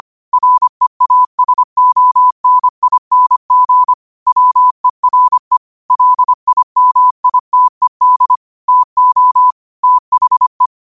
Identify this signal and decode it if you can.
Morse code, heard as a keyed tone, about 25 wpm, 1000 Hz. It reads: REASONING WERE LIMITED TO THE